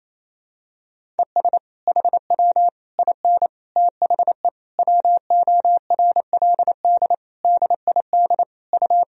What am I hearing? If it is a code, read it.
Morse code, heard as a keyed tone, about 28 wpm, 700 Hz. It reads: ES HW IN THE WORLD DID U
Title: Sentences from Top 200 Words and Abbreviations 28wpm